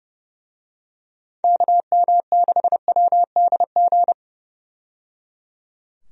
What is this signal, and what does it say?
Morse code, heard as a keyed tone, about 30 wpm, 700 Hz. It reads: KM6WDG
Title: USA Call Signs - 6 Characters - 30wpm